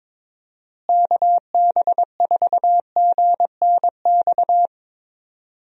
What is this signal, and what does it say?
Morse code, heard as a keyed tone, about 22 wpm, 700 Hz. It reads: KB4GNX
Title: USA Call Signs - 6 Characters - Repeated 2x - 22wpm